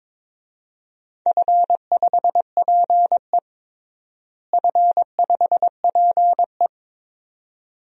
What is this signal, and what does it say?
Morse code, heard as a keyed tone, about 22 wpm, 700 Hz. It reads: F5PE F5PE
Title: DXCC-100 Call Signs - 4 Characters - Repeated 2x - 22wpm